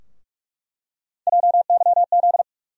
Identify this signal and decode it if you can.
Morse code, heard as a keyed tone, about 42 wpm, 700 Hz. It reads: JYZ
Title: USA Call Sign Postfixes - 3 Letters 45wpm